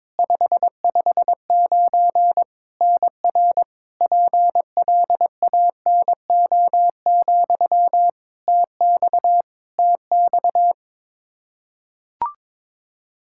Morse code, heard as a keyed tone, about 22 wpm, 700 Hz. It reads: 559 NR PLANO, TX TX
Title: Practice QSO - Exchange 1 - Beginning - 22wpm